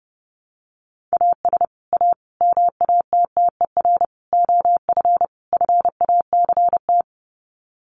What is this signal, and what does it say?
Morse code, heard as a keyed tone, about 30 wpm, 700 Hz. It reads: AS A MATTER OF FACT
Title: Sets of 5 Words 30wpm